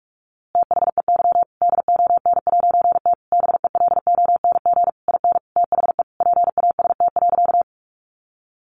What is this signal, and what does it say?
Morse code, heard as a keyed tone, about 46 wpm, 700 Hz. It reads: THEY DON'T BELONG IN THE PAST.
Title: A Sound Of Thunder 45wpm